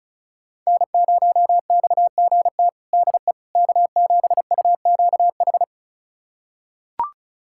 Morse code, heard as a keyed tone, about 36 wpm, 700 Hz. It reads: N0XGT DE K7UQH
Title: USA Callsign DE Callsign - 5 Characters - 35wpm